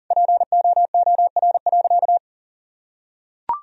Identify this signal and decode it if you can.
Morse code, heard as a keyed tone, about 37 wpm, 700 Hz. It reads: POOR.
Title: Sentences from Top 1000 Words - Review Part 3 - 40wpm